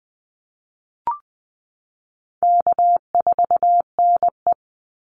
Morse code, heard as a keyed tone, about 20 wpm, 700 Hz. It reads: K4NE
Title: Easy Call Signs - 20wpm